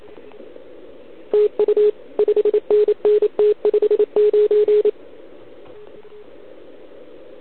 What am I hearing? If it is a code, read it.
Morse code, heard as a keyed tone, about 25 wpm, 410 Hz. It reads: TU 5NN T59
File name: ARCHIV / OKDX2020 / OL4Wa